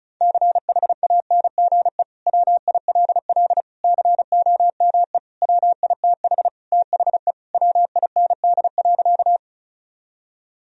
Morse code, heard as a keyed tone, about 36 wpm, 700 Hz. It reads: CHANGE WILL COME WITH THE WIND.
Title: Sentences from Top 1000 Words - Review Part 2 - 35wpm